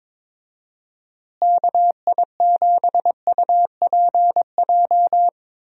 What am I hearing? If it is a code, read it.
Morse code, heard as a keyed tone, about 21 wpm, 700 Hz. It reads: KI7UPJ
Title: USA Call Signs - 6 Characters - Repeated 2x - 22wpm